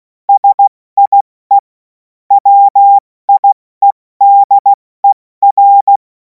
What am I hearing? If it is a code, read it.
Morse code, heard as a keyed tone, about 16 wpm, 800 Hz. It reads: SIE WIEDER
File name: Morsekurs / Witze / Witz 11. August 2026 70 F80